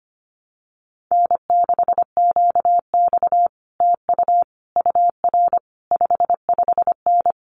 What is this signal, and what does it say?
Morse code, heard as a keyed tone, about 25 wpm, 700 Hz. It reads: N6QX TU UR 55N
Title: Common QSO Phrases Encore 25wpm